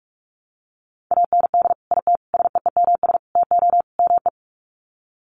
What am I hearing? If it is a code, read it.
Morse code, heard as a keyed tone, about 44 wpm, 700 Hz. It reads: AND IT SEEMS TO ME
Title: Sets of 5 Words Encore2 45wpm